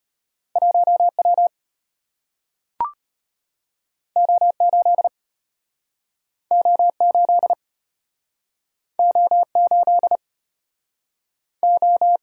Morse code, heard as a keyed tone, about 33 wpm, 700 Hz. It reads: 1W O8 O8 O8 O